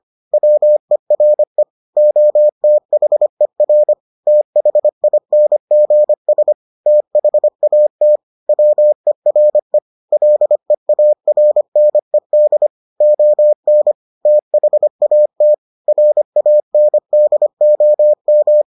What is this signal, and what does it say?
Morse code, heard as a keyed tone, about 25 wpm, 600 Hz. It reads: WERE OTHER THINGS THAT WERE LEARNED ON THAT RANDOM